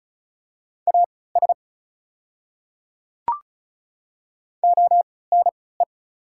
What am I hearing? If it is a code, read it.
Morse code, heard as a keyed tone, about 36 wpm, spaced as for 20 wpm, 690 Hz. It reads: AS ONE
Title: 3x Character Spacing - Top 200 Words - 35wpm